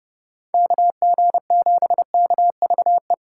Morse code, heard as a keyed tone, about 29 wpm, 700 Hz. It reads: KG7KVE